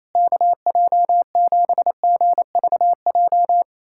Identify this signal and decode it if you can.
Morse code, heard as a keyed tone, about 28 wpm, 700 Hz. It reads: KJ7GVJ
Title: USA Callsign DE Callsign 28wpm